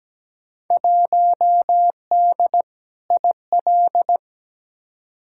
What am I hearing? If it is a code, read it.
Morse code, heard as a keyed tone, about 17 wpm, 700 Hz. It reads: 1D IL
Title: Field Day Exchanges - 17wpm